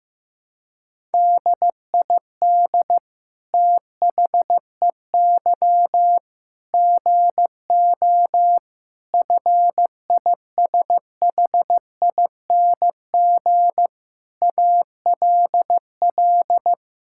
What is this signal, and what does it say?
Morse code, heard as a keyed tone, about 15 wpm, 700 Hz. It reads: DID THEY GO FISHING ALL
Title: Sentences from Top 1000 Words - Review Part 5 - 15wpm